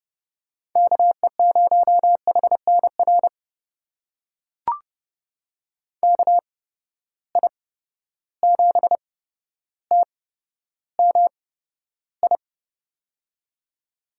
Morse code, heard as a keyed tone, about 29 wpm, spaced as for 13 wpm, 700 Hz. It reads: KE0HNR KI7TMI
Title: USA Call Signs - 6 Characters - 8x Character Spacing - 30wpm